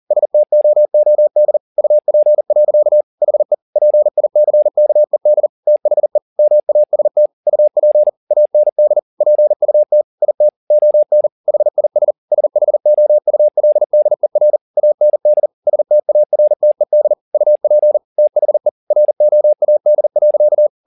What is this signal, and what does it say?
Morse code, heard as a keyed tone, about 40 wpm, 600 Hz. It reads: STOOD UP. HE PICKED THE MAST UP AND PUT IT ON HIS SHOULDER AND STARTED UP THE ROAD.
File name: the old man and the sea 2 40wpm